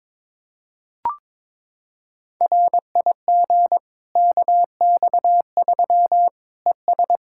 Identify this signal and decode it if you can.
Morse code, heard as a keyed tone, about 22 wpm, 700 Hz. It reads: RIG KX3 ES
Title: Practice QSO - Exchange 2 - Info - 22wpm